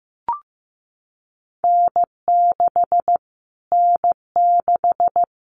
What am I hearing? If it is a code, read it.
Morse code, heard as a keyed tone, about 15 wpm, 700 Hz. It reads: N6 N6
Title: Callsign Prefixes for Top 25 DXCC - Repeated 3x - 15wpm